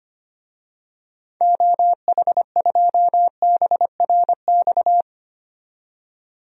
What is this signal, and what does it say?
Morse code, heard as a keyed tone, about 25 wpm, 700 Hz. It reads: OH2BRX
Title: DXCC-100 Call Signs - 6 Characters - 25wpm